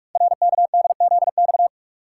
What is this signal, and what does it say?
Morse code, heard as a keyed tone, about 44 wpm, 700 Hz. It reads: RKDZX
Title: Code Groups - 5 Letters - 45wpm